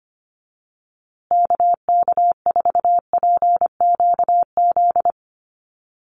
Morse code, heard as a keyed tone, about 25 wpm, 700 Hz. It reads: KK4PQZ
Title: USA Call Signs Repeated 2x 25wpm